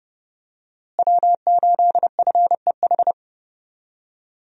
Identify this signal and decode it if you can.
Morse code, heard as a keyed tone, about 30 wpm, 700 Hz. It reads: W8FEH
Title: USA Call Signs - 5 Characters - Repeated 2x - 30wpm